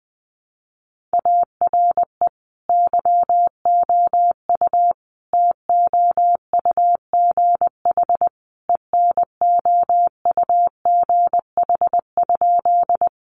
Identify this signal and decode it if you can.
Morse code, heard as a keyed tone, about 20 wpm, 700 Hz. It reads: ARE YOU TOUGH ENOUGH?
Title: Sentences from Top 1200 Words 20wpm